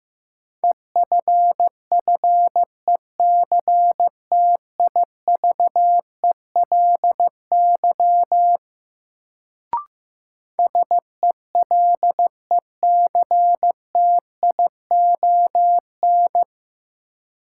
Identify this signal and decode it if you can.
Morse code, heard as a keyed tone, about 15 wpm, 700 Hz. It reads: EFFECTIVELY SELECTION
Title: Sentences from Top 1400 Words - New Word Drill - 15wpm